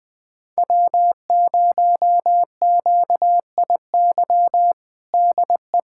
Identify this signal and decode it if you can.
Morse code, heard as a keyed tone, about 21 wpm, 700 Hz. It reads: W0QIY DE
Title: USA Callsign DE Callsign - 5 Characters - 20wpm